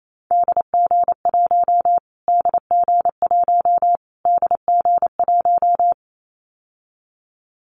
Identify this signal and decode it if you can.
Morse code, heard as a keyed tone, about 28 wpm, 700 Hz. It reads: DG1 DG1 DG1
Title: Callsign Prefixes for Top 25 DXCC - Repeated 3x - 28wpm